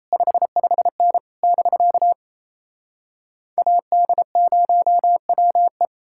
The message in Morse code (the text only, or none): 55N <BK> AD0WE